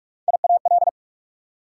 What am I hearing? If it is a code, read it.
Morse code, heard as a keyed tone, about 44 wpm, 690 Hz. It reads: IAL
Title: Prefixes and Suffixes 45wpm